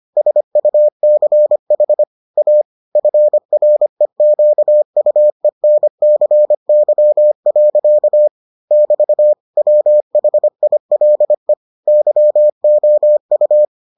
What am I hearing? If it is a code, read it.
Morse code, heard as a keyed tone, about 25 wpm, 600 Hz. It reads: SUCH A FREQUENCY. <BT> WHILE YOU